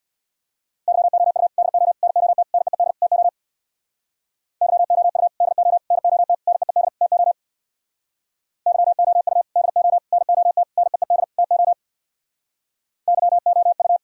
Morse code, heard as a keyed tone, about 64 wpm, 700 Hz. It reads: YOU DO NOT NEED TO YOU DO NOT NEED TO YOU DO NOT NEED TO YOU